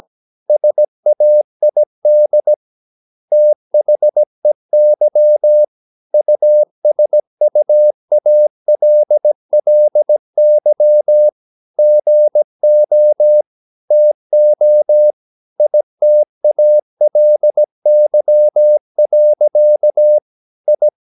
Morse code, heard as a keyed tone, about 17 wpm, 600 Hz. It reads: SAID THEY USUALLY GO TO ITALY. I
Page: Romanen Never af Ken Follet på engelsk.